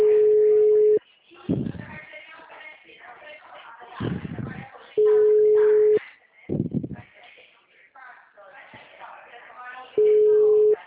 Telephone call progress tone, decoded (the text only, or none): ringback tone